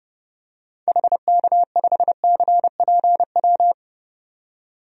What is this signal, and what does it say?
Morse code, heard as a keyed tone, about 30 wpm, 700 Hz. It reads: HK5CPW
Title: DXCC-100 Call Signs - 6 Characters - 30wpm